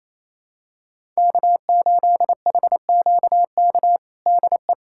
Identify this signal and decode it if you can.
Morse code, heard as a keyed tone, about 28 wpm, 700 Hz. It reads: K8HQK DE
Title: USA Callsign DE Callsign - 5 Characters - 28wpm